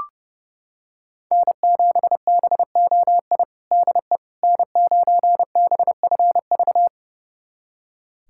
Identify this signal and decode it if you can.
Morse code, heard as a keyed tone, about 32 wpm, 700 Hz. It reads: N7BOI DE N9BFV
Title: USA Callsign DE Callsign - 5 Characters - 30wpm